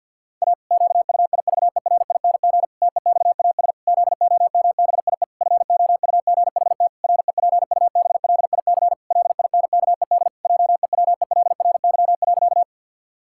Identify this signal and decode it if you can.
Morse code, heard as a keyed tone, about 50 wpm, 700 Hz. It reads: A QUIVERING TEXAS ZOMBIE FOUGHT REPUBLIC LINKED JEWELRY.
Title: Pangram Challenge v2 - 50wpm